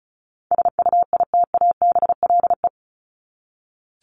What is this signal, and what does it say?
Morse code, heard as a keyed tone, about 34 wpm, 700 Hz. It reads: SUITABLE
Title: Top 1251-1500 Words 35wpm